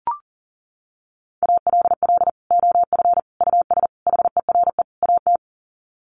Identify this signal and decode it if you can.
Morse code, heard as a keyed tone, about 40 wpm, 700 Hz. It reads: ALL OF US HERE AT
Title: Sets of 5 Words 40wpm